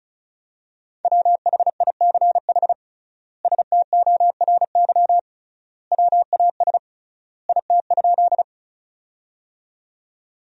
Morse code, heard as a keyed tone, about 36 wpm, 700 Hz. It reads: WHICH STORY WAS IT?
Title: 3x Word-Spacing - Sentences from Top 300 Words - 35wpm